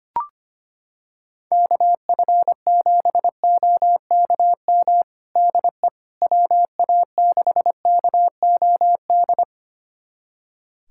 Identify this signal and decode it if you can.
Morse code, heard as a keyed tone, about 25 wpm, 700 Hz. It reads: KF7OKM DE WA6KOD
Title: USA Callsign DE Callsign - 6 Characters - 25wpm